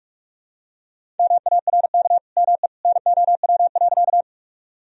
Morse code, heard as a keyed tone, about 46 wpm, 700 Hz. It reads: MARK ME NOW.
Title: Sentences from Top 1000 Words - Review Part 1 - 45wpm